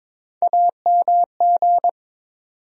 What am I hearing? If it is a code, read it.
Morse code, heard as a keyed tone, about 22 wpm, 700 Hz. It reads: AMG
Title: USA Call Sign Postfixes - 3 Letters 22wpm